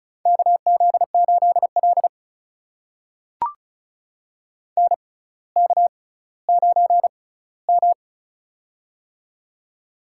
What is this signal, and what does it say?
Morse code, heard as a keyed tone, about 36 wpm, spaced as for 18 wpm, 690 Hz. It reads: KZ8L NK9M
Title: USA Call Signs - 4 Characters - 6x Character Spacing - 35wpm